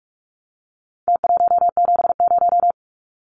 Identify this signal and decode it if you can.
Morse code, heard as a keyed tone, about 44 wpm, 700 Hz. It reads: T170
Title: Numbers - Contest Style - 4-Digit 45wpm